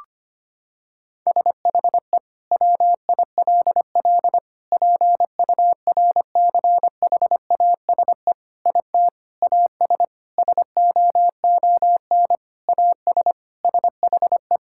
Morse code, heard as a keyed tone, about 25 wpm, 700 Hz. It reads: SHE WILL PURCHASE IT AS SOON AS SHE
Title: Sentences from Top 1000 Words - Review Part 5 - 25wpm